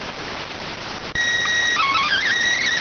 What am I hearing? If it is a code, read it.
SSTV (slow-scan television), PD 290.